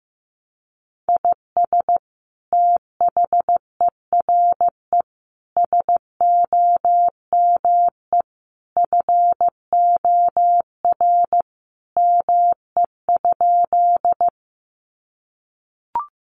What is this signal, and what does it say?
Morse code, heard as a keyed tone, about 15 wpm, 700 Hz. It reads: IS THERE SOME FOR ME?
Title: Sentences from Top 200 Words 15wpm